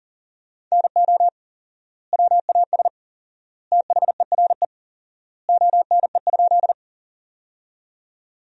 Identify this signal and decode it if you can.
Morse code, heard as a keyed tone, about 40 wpm, 700 Hz. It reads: NO WAS THERE ONE?
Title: 4x Word-Spacing - Sentences from Top 100 Words - 40wpm